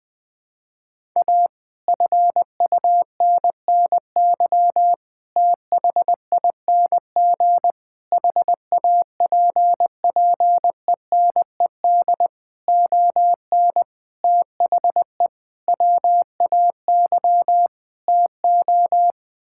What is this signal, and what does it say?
Morse code, heard as a keyed tone, about 20 wpm, 700 Hz. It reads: A FUNNY THING HAPPENED ON THE WAY TO
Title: Sets of 8 Words - 20wpm